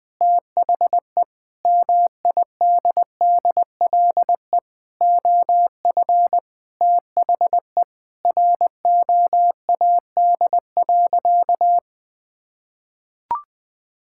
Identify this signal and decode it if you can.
Morse code, heard as a keyed tone, about 20 wpm, 700 Hz. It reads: THE MIDDLE OF THE ROAD.
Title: Sentences with 3-Word Phrases - 20wpm